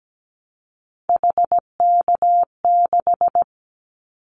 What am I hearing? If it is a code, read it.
Morse code, heard as a keyed tone, about 17 wpm, 700 Hz. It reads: HK6
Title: Callsign Prefixes for Top 100 DXCC - 17wpm